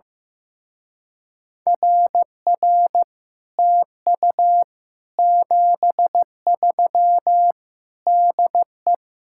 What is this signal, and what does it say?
Morse code, heard as a keyed tone, about 15 wpm, 700 Hz. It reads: RR TU 73 DE
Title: Speed-Racing Common QSO Phrases Encore - 15wpm